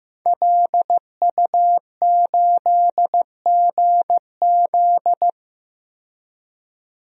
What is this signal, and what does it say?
Morse code, heard as a keyed tone, about 15 wpm, 700 Hz. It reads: LU8GZ
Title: DXCC-100 Call Signs - 5 Characters - 15wpm